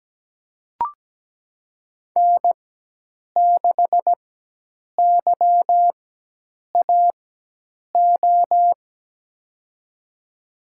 Morse code, heard as a keyed tone, about 17 wpm, spaced as for 8 wpm, 700 Hz. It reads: N6YAO
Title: USA Call Signs - 5 Characters - 4x Character Spacing - 17wpm